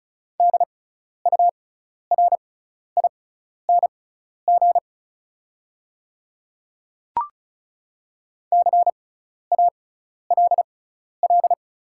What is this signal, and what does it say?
Morse code, heard as a keyed tone, about 36 wpm, spaced as for 12 wpm, 690 Hz. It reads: DURING CALL
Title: 6x Character Spacing - Top 500 Words - 35wpm